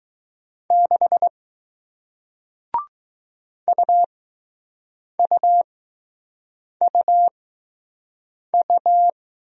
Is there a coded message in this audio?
Yes: Morse code from 0.7 to 9.1 s, heard as a keyed tone, about 20 wpm, spaced as for 6 wpm, 700 Hz. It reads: 6 UUUU